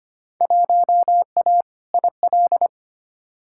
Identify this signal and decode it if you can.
Morse code, heard as a keyed tone, about 25 wpm, 700 Hz. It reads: 1A IL
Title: Field Day Exchanges - 25wpm